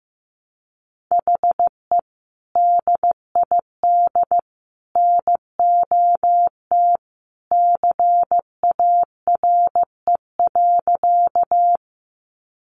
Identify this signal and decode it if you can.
Morse code, heard as a keyed tone, about 15 wpm, 700 Hz. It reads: HE DID NOT CARE.
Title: Sentences from Top 600 Words 15wpm